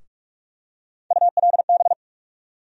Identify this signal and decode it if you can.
Morse code, heard as a keyed tone, about 44 wpm, 700 Hz. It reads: ULB
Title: USA Call Sign Postfixes - 3 Letters 45wpm